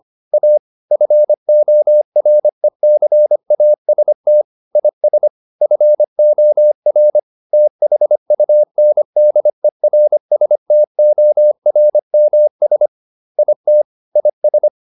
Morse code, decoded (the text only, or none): A FORECAST IS FOR THUNDERSTORMS IT IS